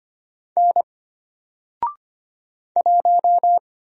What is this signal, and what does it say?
Morse code, heard as a keyed tone, about 25 wpm, 700 Hz. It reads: N 1
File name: Single-Letter-Number-25wpm